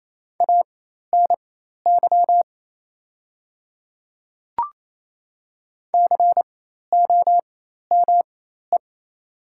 Morse code, heard as a keyed tone, about 28 wpm, spaced as for 13 wpm, 700 Hz. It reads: ANY COME
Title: 4x Character Spacing - Top 300 Words - 28wpm